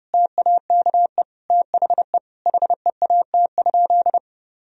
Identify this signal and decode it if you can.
Morse code, heard as a keyed tone, about 30 wpm, 700 Hz. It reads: TAKE THE HEAT?
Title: Sentences from Top 1000 Words - Review Part 3 - 30wpm